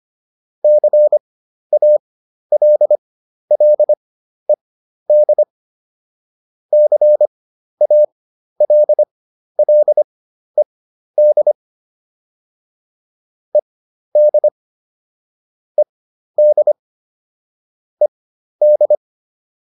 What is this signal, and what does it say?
Morse code, heard as a keyed tone, about 25 wpm, spaced as for 12 wpm, 600 Hz. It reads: CALLED CALLED ED ED ED